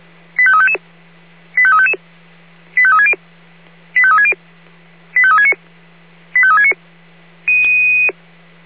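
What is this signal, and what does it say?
Five-tone selective call (ZVEI), heard as quick runs of five tones, tones of about 70 ms. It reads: 85355 85355 85360 85360 85369 85369
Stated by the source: Received on SDR, ZVEI five-tone call.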